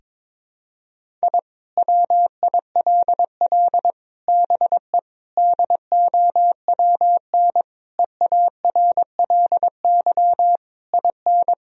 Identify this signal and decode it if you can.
Morse code, heard as a keyed tone, about 21 wpm, 700 Hz. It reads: I WILL BE DOWN EARLY IN
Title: Sentences from Top 1000 Words - Review Part 2 - 22wpm